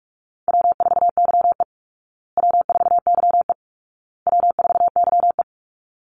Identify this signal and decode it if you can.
Morse code, heard as a keyed tone, about 44 wpm, 700 Hz. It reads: W4YE W4YE W4YE
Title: Easy Call Signs Repeated 3x - 45wpm